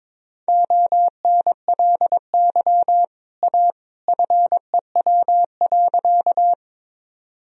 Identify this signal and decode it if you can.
Morse code, heard as a keyed tone, about 22 wpm, 700 Hz. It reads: ONLY A FEW.
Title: Sentences from Top 1000 Words - Review Part 1 - 22wpm